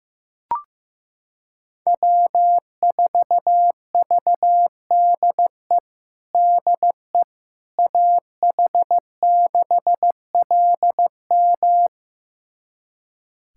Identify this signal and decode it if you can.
Morse code, heard as a keyed tone, about 15 wpm, 700 Hz. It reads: W4VDE DE AH6LM
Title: USA Callsign DE Callsign - 5 Characters - 15wpm